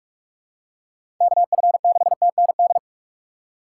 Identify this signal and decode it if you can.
Morse code, heard as a keyed tone, about 41 wpm, 700 Hz. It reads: KF6TND
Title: USA Call Signs - 6 Characters - 4x Character Spacing - 45wpm